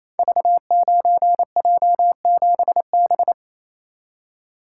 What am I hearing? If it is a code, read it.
Morse code, heard as a keyed tone, about 27 wpm, 700 Hz. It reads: V9J7B
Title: Code Groups - 5 Letter:Numbers - 28wpm